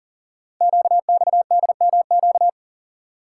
Morse code, heard as a keyed tone, about 38 wpm, 690 Hz. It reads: QXDMQ
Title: Code Groups - 5 Letters - 40wpm